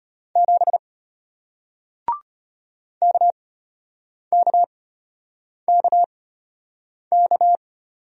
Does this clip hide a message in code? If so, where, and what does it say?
Yes: Morse code from 0.4 to 7.6 s, heard as a keyed tone, about 33 wpm, spaced as for 8 wpm, 700 Hz. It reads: 7 KKKK